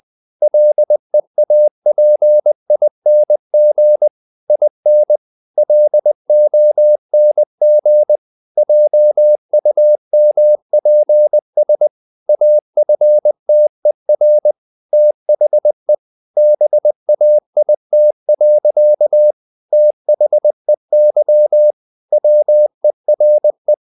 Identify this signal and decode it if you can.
Morse code, heard as a keyed tone, about 20 wpm, 600 Hz. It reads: LEAPING IN LONG JUMPS AFTER THE BAIT. THEY WERE
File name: the old man and the sea / the old man and the sea part 1 20wpm